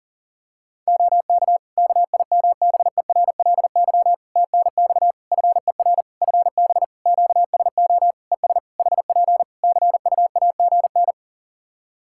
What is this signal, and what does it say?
Morse code, heard as a keyed tone, about 40 wpm, 700 Hz. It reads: OK KIMBERLY TNX FER FB QSO ES HP CUAGN
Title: Practice QSO - Exchange 4 - Ending - 40wpm